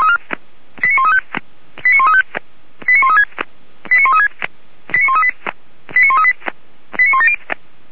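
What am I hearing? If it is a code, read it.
Five-tone selective call (ZVEI), heard as quick runs of five tones, tones of about 70 ms. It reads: 89135 89135 89136 89136 89138 89138 89170
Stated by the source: ZVEI five-tone call 89133 received by SDR, archived by Feuerwehr Orlamünde fire brigade.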